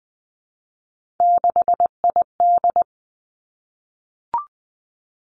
Morse code, heard as a keyed tone, about 19 wpm, 700 Hz. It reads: 6ID
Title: Three Challenging Letter-Number Combinations 20wpm